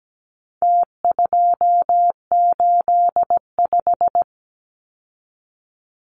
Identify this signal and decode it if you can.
Morse code, heard as a keyed tone, about 17 wpm, 700 Hz. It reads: T285
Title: Numbers - Contest Style - 4-Digit 17wpm